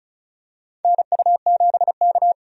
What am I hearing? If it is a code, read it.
Morse code, heard as a keyed tone, about 36 wpm, 700 Hz. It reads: NU7K
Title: USA Call Signs - 4 Characters - 35wpm